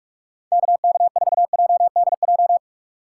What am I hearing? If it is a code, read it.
Morse code, heard as a keyed tone, about 42 wpm, 700 Hz. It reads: KK4JDJ
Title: USA Call Signs - 4x Character Spacing - 45wpm